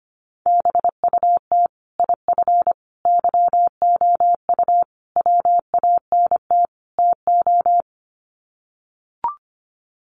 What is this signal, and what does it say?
Morse code, heard as a keyed tone, about 25 wpm, 700 Hz. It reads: BUT IF YOU WANT TO
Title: Sets of 5 Words Encore2 25wpm